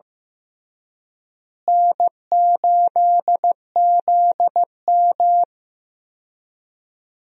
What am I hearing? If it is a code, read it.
Morse code, heard as a keyed tone, about 15 wpm, 700 Hz. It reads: N8ZM
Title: USA Call Signs - 4 Characters - Repeated 3x - 15wpm